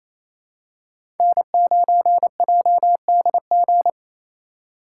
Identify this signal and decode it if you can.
Morse code, heard as a keyed tone, about 27 wpm, 700 Hz. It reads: N9JDG
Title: USA Call Signs - 3x Character Spacing - 28wpm